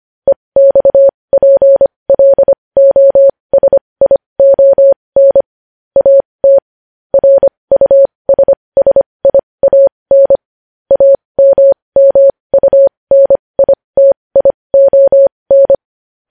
Morse code, heard as a keyed tone, about 25 wpm, spaced as for 20 wpm, 550 Hz. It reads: EXPLOSION AT RUSSIAN AMMUNITION
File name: podcast / News headlines-1x-25x20